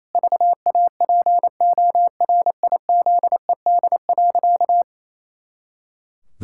Morse code, heard as a keyed tone, about 28 wpm, 700 Hz. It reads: VAPORIZED.
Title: Pangram Challenge v2 - 28wpm